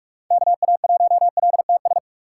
Morse code, heard as a keyed tone, about 48 wpm, 700 Hz. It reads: KA1LTS
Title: USA Call Signs - 6 Characters - 4x Character Spacing - 45wpm